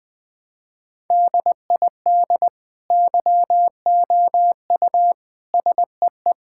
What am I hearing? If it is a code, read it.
Morse code, heard as a keyed tone, about 21 wpm, 700 Hz. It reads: DID YOU SEE
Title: Sentences from Top 1000 Words - Review Part 5 - 20wpm